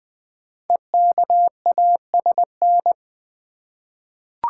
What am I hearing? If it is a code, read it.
Morse code, heard as a keyed tone, about 20 wpm, 700 Hz. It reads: EKASN
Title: Code Groups - 5 Letters - 20wpm